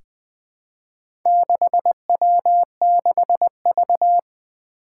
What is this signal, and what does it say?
Morse code, heard as a keyed tone, about 20 wpm, 700 Hz. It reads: 6W6V
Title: Four Challenging Letter-Number Combinations 20wpm